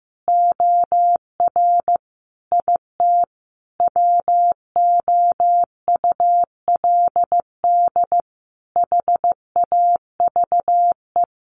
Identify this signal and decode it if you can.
Morse code, heard as a keyed tone, about 15 wpm, 700 Hz. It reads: OR IT WOULD HAVE
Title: Sentences from Top 1500 Words 15wpm